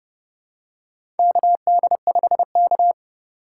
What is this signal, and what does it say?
Morse code, heard as a keyed tone, about 28 wpm, 700 Hz. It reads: KD5K
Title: USA Call Signs - 4 Characters - 3x Character Spacing - 30wpm